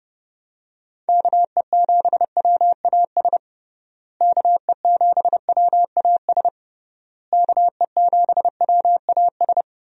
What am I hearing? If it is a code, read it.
Morse code, heard as a keyed tone, about 30 wpm, 700 Hz. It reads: KE7WAS KE7WAS KE7WAS
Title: USA Call Signs - 6 Characters - Repeated 3x - 30wpm